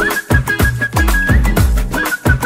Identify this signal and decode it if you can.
There is no code.